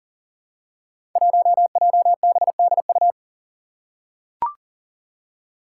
Morse code, heard as a keyed tone, about 40 wpm, 700 Hz. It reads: 1JBDU
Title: Five Challenging Letter-Number Combinations 40wpm